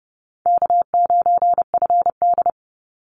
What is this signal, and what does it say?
Morse code, heard as a keyed tone, about 30 wpm, 700 Hz. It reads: K9FD
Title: Easy Call Signs Repeated 2x - 30wpm